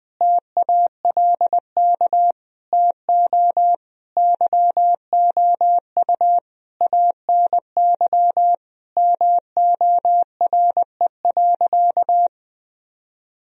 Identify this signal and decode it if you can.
Morse code, heard as a keyed tone, about 20 wpm, 700 Hz. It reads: TALK TO YOU ANY MORE.
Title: Sentences from Top 1000 Words - Review Part 1 - 20wpm